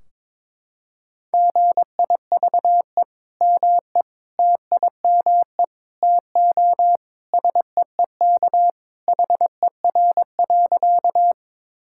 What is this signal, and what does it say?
Morse code, heard as a keyed tone, about 22 wpm, 700 Hz. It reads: GIVE ME TIME TO SEEK HER.
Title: Sentences from Top 1600 Words 22wpm